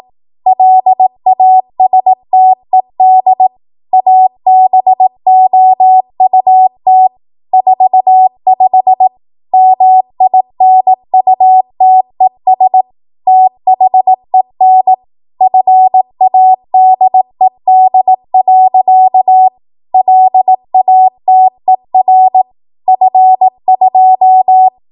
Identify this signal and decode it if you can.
Morse code, heard as a keyed tone, about 18 wpm, 750 Hz. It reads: LASTED ABOUT 45 MINUTES THEN FADED. LATER F2
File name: Archive / 18 WPM / 18WPM